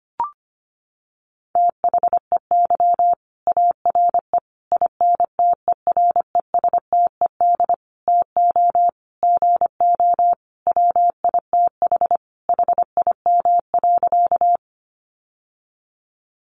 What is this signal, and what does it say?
Morse code, heard as a keyed tone, about 25 wpm, 700 Hz. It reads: THEY ARE INTERESTED TO GO WITH HIM.
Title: Sentences from Top 900 Words 25wpm